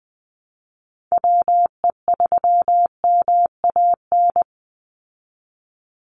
Morse code, heard as a keyed tone, about 20 wpm, 700 Hz. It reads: WE3MAN
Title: USA Call Signs 20wpm